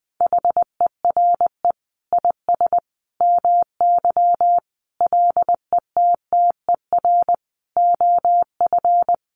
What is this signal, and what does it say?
Morse code, heard as a keyed tone, about 20 wpm, 700 Hz. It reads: HERE IS MY LETTER OF
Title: Sentences from Top 1400 Words 20wpm